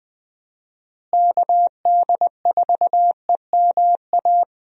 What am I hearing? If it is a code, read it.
Morse code, heard as a keyed tone, about 20 wpm, 700 Hz. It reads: KD4EMA
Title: USA Call Signs - 6 Characters - Repeated 2x - 20wpm